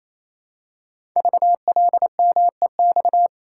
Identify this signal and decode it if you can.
Morse code, heard as a keyed tone, about 27 wpm, 700 Hz. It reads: VLMEX
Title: Code Groups - 5 Letters - 28wpm